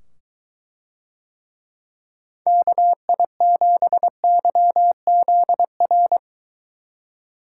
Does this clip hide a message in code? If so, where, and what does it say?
Yes: Morse code from 2.5 to 6.2 s, heard as a keyed tone, about 23 wpm, 700 Hz. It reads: KI7YZR